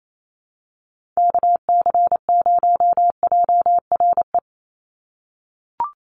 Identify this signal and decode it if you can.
Morse code, heard as a keyed tone, about 28 wpm, 700 Hz. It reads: KC0JRE
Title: USA Call Signs 28wpm